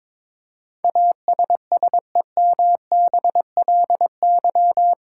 Morse code, heard as a keyed tone, about 21 wpm, 700 Hz. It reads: ASSEMBLY
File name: practice / 22wpm / Eight Letter Words - 22wpm